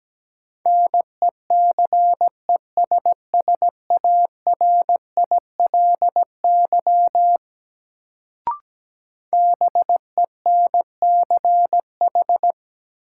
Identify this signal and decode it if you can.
Morse code, heard as a keyed tone, about 17 wpm, 700 Hz. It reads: NECESSARILY BENCH
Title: Sentences from Top 1400 Words - New Word Drill - 17wpm